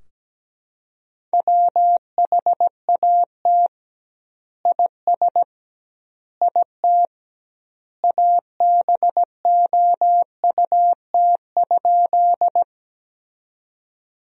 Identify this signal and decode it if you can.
Morse code, heard as a keyed tone, about 17 wpm, 700 Hz. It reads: WHAT IS IT ABOUT?
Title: 2x Word-Spacing - Sentences from Top 100 Words - 17wpm